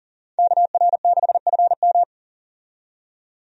Morse code, heard as a keyed tone, about 40 wpm, 700 Hz. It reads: KR6FM
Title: USA Call Signs - 5 Characters - 40wpm